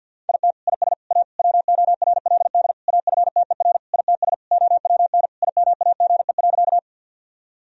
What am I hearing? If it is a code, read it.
Morse code, heard as a keyed tone, about 50 wpm, 700 Hz. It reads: IT IS A WORLD AFTER ITS OWN IMAGE.
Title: Sentences from Top 1000 Words - Review Part 4 - 50wpm